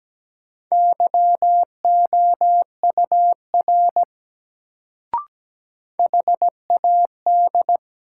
Morse code, heard as a keyed tone, about 17 wpm, 700 Hz. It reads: YOUR HAD
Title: Sets of 2 Words - New Word Drill - 17wpm